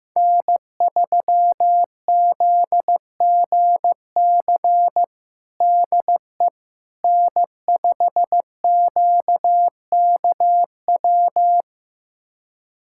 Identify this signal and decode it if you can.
Morse code, heard as a keyed tone, about 15 wpm, 700 Hz. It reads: N3ZGC DE N5QKW
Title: USA Callsign DE Callsign - 5 Characters - 15wpm